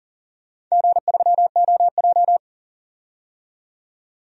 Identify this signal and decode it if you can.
Morse code, heard as a keyed tone, about 37 wpm, 700 Hz. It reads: G3OJ
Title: DXCC-100 Call Signs - 4 Characters - 40wpm